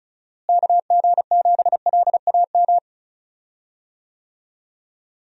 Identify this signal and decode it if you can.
Morse code, heard as a keyed tone, about 36 wpm, 700 Hz. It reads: KG7LAM
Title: USA Call Signs - 6 Characters - Repeated 3x - 35wpm